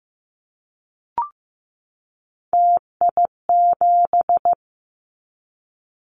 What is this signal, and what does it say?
Morse code, heard as a keyed tone, about 15 wpm, 700 Hz. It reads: TI7
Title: Callsign Prefixes for Top 100 DXCC - 15wpm